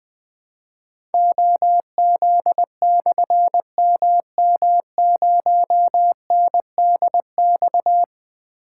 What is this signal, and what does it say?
Morse code, heard as a keyed tone, about 20 wpm, 700 Hz. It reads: OZ/MM0NDX
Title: Difficult Call Signs 20wpm